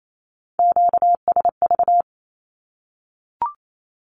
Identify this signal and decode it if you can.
Morse code, heard as a keyed tone, about 28 wpm, 700 Hz. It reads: QSV
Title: Q-Codes Repeated 3x 28wpm